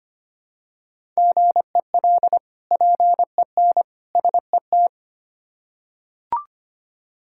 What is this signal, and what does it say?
Morse code, heard as a keyed tone, about 25 wpm, 700 Hz. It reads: GEL PEN SET
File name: practice / 25wpm / Difficult Words with Hints - 25wpm